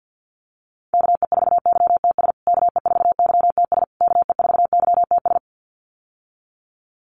Morse code, heard as a keyed tone, about 50 wpm, 700 Hz. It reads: KE4YTS KE4YTS KE4YTS
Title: USA Call Signs Repeated 3x 50wpm